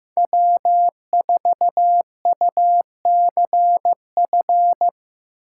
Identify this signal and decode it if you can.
Morse code, heard as a keyed tone, about 18 wpm, 700 Hz. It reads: W4UCF
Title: Speed Racing - USA 1x3 Call Signs 15wpm